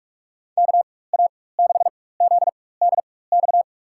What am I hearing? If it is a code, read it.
Morse code, heard as a keyed tone, about 47 wpm, spaced as for 21 wpm, 700 Hz. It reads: KA6ZDX
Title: USA Call Signs - 6 Characters - 4x Character Spacing - 45wpm